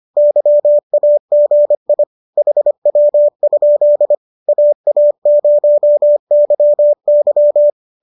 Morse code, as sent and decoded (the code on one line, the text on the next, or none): -.-- .- --. .. / .... .-- ..--.. / .- .- ----- -.-- -.--
YAGI HW? AA0YY